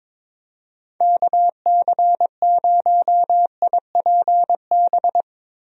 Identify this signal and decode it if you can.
Morse code, heard as a keyed tone, about 22 wpm, 700 Hz. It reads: KC0IPB
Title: USA Call Signs - 6 Characters - Repeated 2x - 22wpm